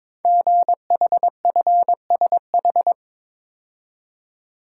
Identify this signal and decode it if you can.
Morse code, heard as a keyed tone, about 22 wpm, 700 Hz. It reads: GHFSH
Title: Code Groups - 5 Letters - 22wpm